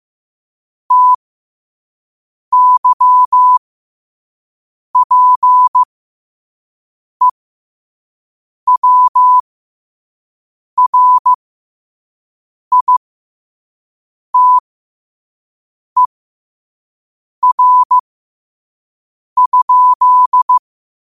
Morse code, heard as a keyed tone, about 15 wpm, spaced as for 5 wpm, 1000 Hz. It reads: TYPEWRITER?
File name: qotdfiles / 5wpm